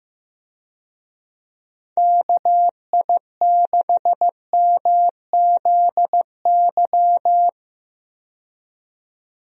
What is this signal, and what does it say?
Morse code, heard as a keyed tone, about 15 wpm, 700 Hz. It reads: KI6MZY
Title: USA Call Signs - 6 Characters - 15wpm